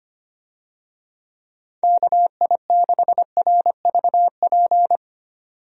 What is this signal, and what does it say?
Morse code, heard as a keyed tone, about 25 wpm, 700 Hz. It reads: KI6RVP
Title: USA Call Signs - 6 Characters - 25wpm